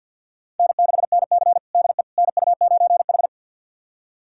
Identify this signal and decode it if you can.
Morse code, heard as a keyed tone, about 52 wpm, 700 Hz. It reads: N6NC DE NU9H
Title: USA Callsign DE Callsign - 4 Characters - 50wpm